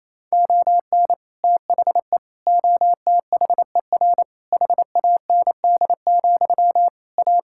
Morse code, heard as a keyed tone, about 27 wpm, 700 Hz. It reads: ON THE OTHER HAND, A
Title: Sentences from Top 1600 Words 28wpm